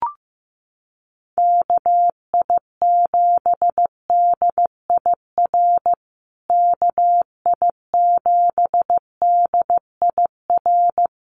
Morse code, heard as a keyed tone, about 15 wpm, 700 Hz. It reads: KI7DIR KI7DIR
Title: USA Call Signs Repeated 3x 15wpm